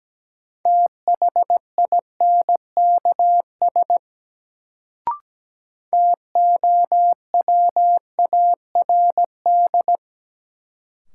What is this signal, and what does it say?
Morse code, heard as a keyed tone, about 17 wpm, 700 Hz. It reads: THINKS TOWARD
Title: Sentences from Top 1000 Words - Review Part 6 - New Word Drill - 17wpm